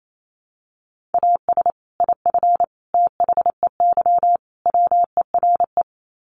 Morse code, heard as a keyed tone, about 28 wpm, 700 Hz. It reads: AS IF THEY WERE
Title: Sets of 4 Words 28wpm